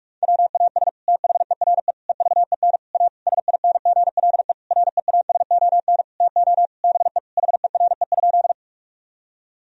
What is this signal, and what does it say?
Morse code, heard as a keyed tone, about 46 wpm, 700 Hz. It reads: WAS THERE EVEN A SINGLE REASON TO BE HERE?
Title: Sentences from Top 1000 Words - Review Part 4 - 45wpm